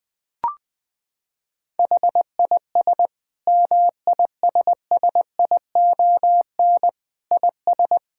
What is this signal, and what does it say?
Morse code, heard as a keyed tone, about 21 wpm, 700 Hz. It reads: HIS MISSION IS
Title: Sentences from Top 1600 Words 20wpm